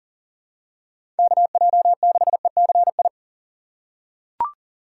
Morse code, heard as a keyed tone, about 40 wpm, 700 Hz. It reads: KJ6ECI
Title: USA Call Signs - 6 Characters - 3x Character Spacing - 40wpm